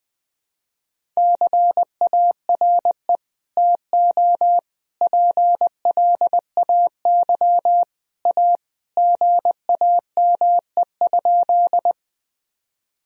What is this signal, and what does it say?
Morse code, heard as a keyed tone, about 20 wpm, 700 Hz. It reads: CARE TO PLAY A GAME?
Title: Sentences from Top 1000 Words - Review Part 1 - 20wpm